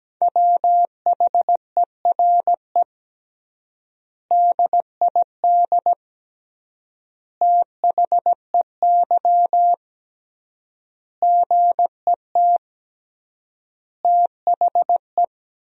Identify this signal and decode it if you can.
Morse code, heard as a keyed tone, about 17 wpm, 700 Hz. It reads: WHERE DID THEY GET THE
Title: 3x Word-Spacing - Sentences from Top 400 Words - 17wpm